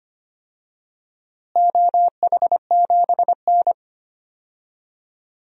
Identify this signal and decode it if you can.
Morse code, heard as a keyed tone, about 25 wpm, 700 Hz. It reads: OH7N
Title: DXCC-100 Call Signs - 4 Characters - Repeated 3x - 25wpm